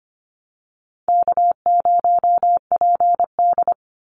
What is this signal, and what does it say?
Morse code, heard as a keyed tone, about 25 wpm, 700 Hz. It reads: K0PD
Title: Call Signs - 25wpm